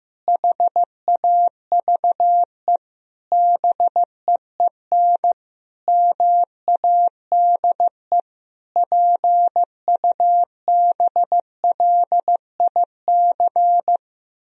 Morse code, heard as a keyed tone, about 15 wpm, 700 Hz. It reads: HAVE BEEN MADE PUBLIC
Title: Sentences from Top 1000 Words - Review Part 3 - 15wpm